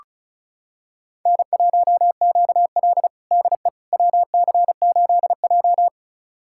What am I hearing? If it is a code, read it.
Morse code, heard as a keyed tone, about 36 wpm, 700 Hz. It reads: N1QL DE WC8J
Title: USA Callsign DE Callsign - 4 Characters - 35wpm